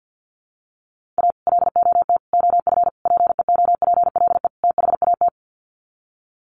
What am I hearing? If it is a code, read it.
Morse code, heard as a keyed tone, about 50 wpm, 700 Hz. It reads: A LOT OF PEOPLE THAT
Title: Sets of 5 Words Encore2 50wpm